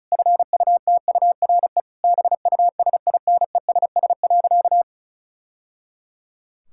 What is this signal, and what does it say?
Morse code, heard as a keyed tone, about 36 wpm, 700 Hz. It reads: FUTURE BUSINESS.
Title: Sentences from Top 1000 Words - Review Part 6 - 35wpm